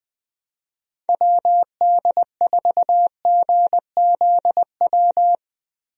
Morse code, heard as a keyed tone, about 20 wpm, 700 Hz. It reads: WD4GZW
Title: USA Call Signs - 6 Characters - Repeated 3x - 20wpm